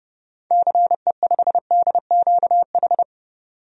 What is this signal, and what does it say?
Morse code, heard as a keyed tone, about 32 wpm, 700 Hz. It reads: CE5DQH
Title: DXCC-100 Call Signs - 6 Characters - 30wpm